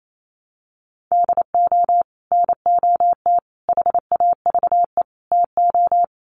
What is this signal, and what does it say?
Morse code, heard as a keyed tone, about 28 wpm, 700 Hz. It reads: DO NOT HAVE TO
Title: Sets of 4 Words 28wpm